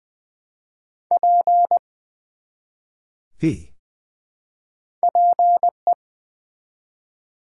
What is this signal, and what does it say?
Morse code, heard as a keyed tone, about 20 wpm, 700 Hz. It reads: P PE
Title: Build A Word - 1 Letter At A Time - 3-Letter Words - 20wpm